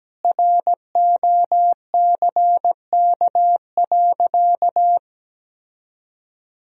Morse code, heard as a keyed tone, about 17 wpm, 700 Hz. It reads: ROCK.
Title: Sentences from Top 1000 Words - Review Part 2 - 17wpm